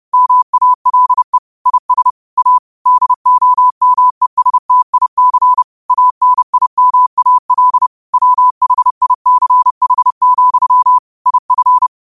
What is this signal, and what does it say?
Morse code, heard as a keyed tone, about 30 wpm, 1000 Hz. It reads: MALE IS A DOMESTIC ANIMAL WHICH, IF
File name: qotdfiles / 30wpm